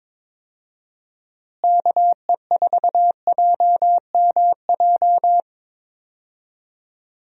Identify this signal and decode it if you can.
Morse code, heard as a keyed tone, about 22 wpm, 700 Hz. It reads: KE4JMJ
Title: USA Call Signs - 6 Characters - Repeated 2x - 22wpm